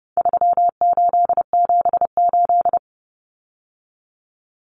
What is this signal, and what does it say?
Morse code, heard as a keyed tone, about 30 wpm, 700 Hz. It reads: 3878
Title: Numbers - 4-Digit 30wpm